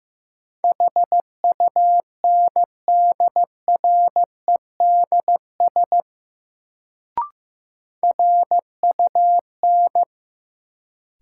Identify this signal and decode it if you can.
Morse code, heard as a keyed tone, about 15 wpm, 700 Hz. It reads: HUNDREDS RUN
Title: Sets of 4 Words - New Word Drill - 15wpm